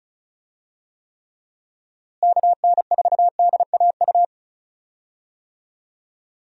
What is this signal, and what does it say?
Morse code, heard as a keyed tone, about 34 wpm, 700 Hz. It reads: KN4DAU
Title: USA Call Signs - 6 Characters - Repeated 2x - 35wpm